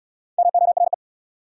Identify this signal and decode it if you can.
Morse code, heard as a keyed tone, about 70 wpm, 690 Hz. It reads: MORE